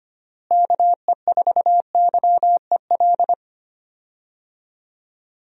Morse code, heard as a keyed tone, about 25 wpm, 700 Hz. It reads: KE4YEL
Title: USA Call Signs - 6 Characters - Repeated 3x - 25wpm